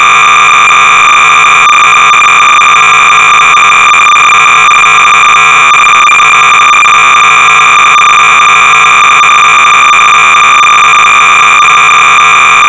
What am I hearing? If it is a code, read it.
Cassette data (Kansas City type).
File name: BASIC-Sorcerer-300baud